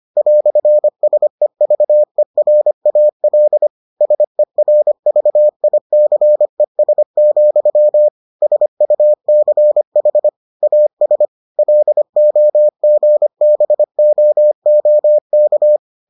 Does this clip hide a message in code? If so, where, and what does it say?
Yes: Morse code from 0.2 to 15.8 s, heard as a keyed tone, about 25 wpm, 600 Hz. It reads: "SEVERAL SERVICES, SUCH AS LOGBOOK